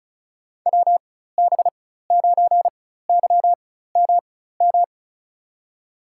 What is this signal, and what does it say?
Morse code, heard as a keyed tone, about 34 wpm, spaced as for 16 wpm, 700 Hz. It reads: WB9YMM
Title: USA Call Signs - 6 Characters - 4x Character Spacing - 35wpm